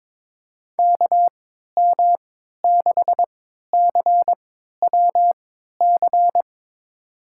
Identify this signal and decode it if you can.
Morse code, heard as a keyed tone, about 22 wpm, spaced as for 12 wpm, 700 Hz. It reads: KM6CWC
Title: USA Call Signs - 6 Characters - 3x Character Spacing - 22wpm